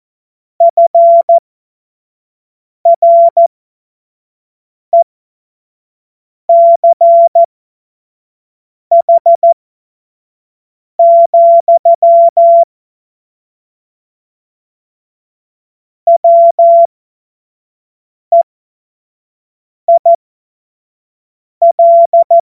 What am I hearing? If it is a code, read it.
Morse code, heard as a keyed tone, about 14 wpm, spaced as for 5 wpm, 680 Hz. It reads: FRECH, WEIL
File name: Witz-15-Juli-2025-25-F70